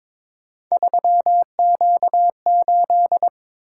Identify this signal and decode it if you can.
Morse code, heard as a keyed tone, about 22 wpm, 700 Hz. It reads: 3Q8
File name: practice / 22wpm / Three Letter-Number Combinations - 22wpm